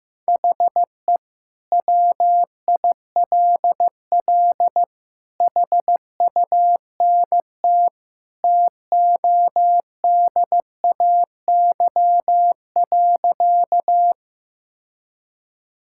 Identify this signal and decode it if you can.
Morse code, heard as a keyed tone, about 14 wpm, 700 Hz. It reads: HE WILL HUNT TODAY.
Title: Sentences from Top 1600 Words 15wpm